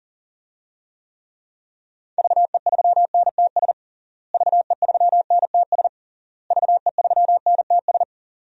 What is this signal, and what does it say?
Morse code, heard as a keyed tone, about 37 wpm, 700 Hz. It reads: VE3NTS VE3NTS VE3NTS
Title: DXCC-100 Call Signs - 6 Characters - Repeated 3x - 40wpm